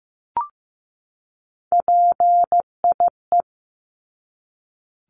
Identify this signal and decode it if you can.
Morse code, heard as a keyed tone, about 15 wpm, 700 Hz. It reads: PIE
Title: Top 1751-2000 Words 15wpm